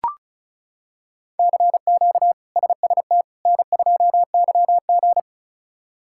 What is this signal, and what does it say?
Morse code, heard as a keyed tone, about 36 wpm, 700 Hz. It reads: CQ SST N2YG
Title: K1USN SST QSO Practice - 35wpm